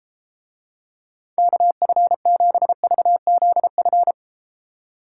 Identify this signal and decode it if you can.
Morse code, heard as a keyed tone, about 30 wpm, 700 Hz. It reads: KF7VZF